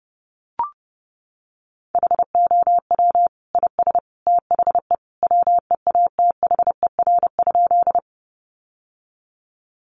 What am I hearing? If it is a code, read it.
Morse code, heard as a keyed tone, about 30 wpm, 700 Hz. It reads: HOW IS THE WEATHER?
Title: Sentences from Top 800 Words 30wpm